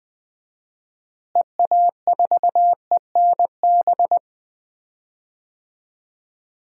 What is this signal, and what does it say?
Morse code, heard as a keyed tone, about 21 wpm, 700 Hz. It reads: EA4ENB
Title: DXCC-100 Call Signs - 6 Characters - Repeated 3x - 20wpm